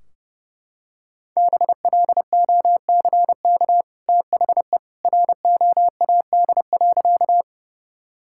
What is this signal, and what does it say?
Morse code, heard as a keyed tone, about 32 wpm, 700 Hz. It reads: BLOCK THE ROAD.
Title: Sentences from Top 1600 Words 30wpm